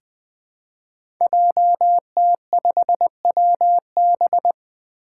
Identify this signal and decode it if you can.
Morse code, heard as a keyed tone, about 20 wpm, 700 Hz. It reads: JT5WB
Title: Five Challenging Letter-Number Combinations 20wpm